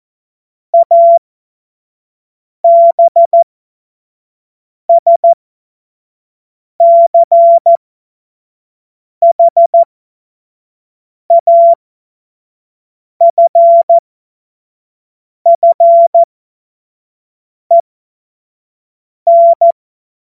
Morse code, heard as a keyed tone, about 14 wpm, spaced as for 5 wpm, 680 Hz. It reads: ABSCHAFFEN
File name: Witz-12-August-2025-25-F70